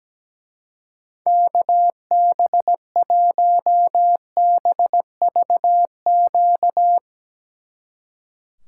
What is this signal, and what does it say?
Morse code, heard as a keyed tone, about 18 wpm, 700 Hz. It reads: KB1BVQ
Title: Speed Racing - USA 2x3 Call Signs 17wpm